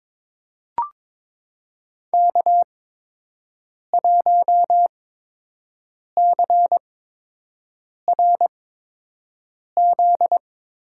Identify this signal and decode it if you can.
Morse code, heard as a keyed tone, about 22 wpm, spaced as for 6 wpm, 700 Hz. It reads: K1CRZ
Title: USA Call Signs - 5 Characters - 8x Character Spacing - 22wpm